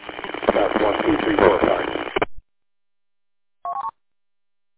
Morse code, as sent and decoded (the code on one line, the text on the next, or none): none